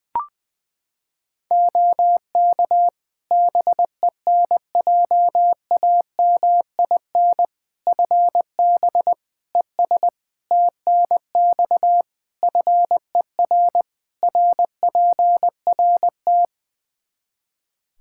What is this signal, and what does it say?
Morse code, heard as a keyed tone, about 20 wpm, 700 Hz. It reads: OK BENJAMIN FB ES TNX FER RPRT
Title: K1USN SST QSO Practice 20wpm